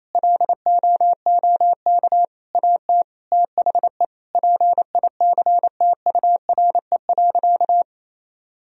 Morse code, heard as a keyed tone, about 28 wpm, 700 Hz. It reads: LOOK AT THE PICTURE.
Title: Sentences from Top 1000 Words - Review Part 1 - 28wpm